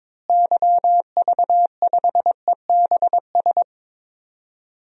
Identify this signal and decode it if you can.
Morse code, heard as a keyed tone, about 22 wpm, 700 Hz. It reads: YV5EBS
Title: DXCC-100 Call Signs - 6 Characters - Repeated 2x - 22wpm